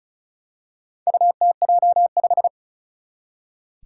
Morse code, heard as a keyed tone, about 36 wpm, 700 Hz. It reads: UTJ5
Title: Four Challenging Letter-Number Combinations 35wpm